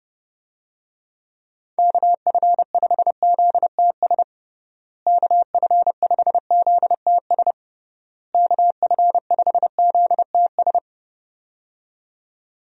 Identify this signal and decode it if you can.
Morse code, heard as a keyed tone, about 30 wpm, 700 Hz. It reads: KF5ZTS KF5ZTS KF5ZTS
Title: USA Call Signs - 6 Characters - Repeated 3x - 30wpm